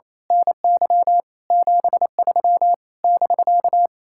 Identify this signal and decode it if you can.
Morse code, heard as a keyed tone, about 26 wpm, 700 Hz. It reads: NY 73 <BK>
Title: Speed-Racing Common QSO Phrases - 28wpm